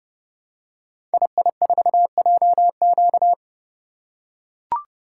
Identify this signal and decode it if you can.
Morse code, heard as a keyed tone, about 30 wpm, 700 Hz. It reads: II4JQ
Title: Code Groups - 5 Letter:Numbers - 30wpm